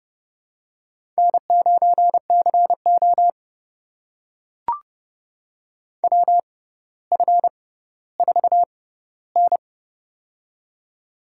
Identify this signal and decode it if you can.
Morse code, heard as a keyed tone, about 30 wpm, spaced as for 15 wpm, 700 Hz. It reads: N9CO WF4N
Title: USA Call Signs - 4 Characters - 6x Character Spacing - 30wpm